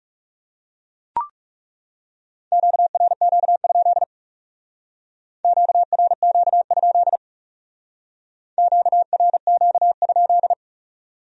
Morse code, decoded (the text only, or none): QRQ? QRQ? QRQ?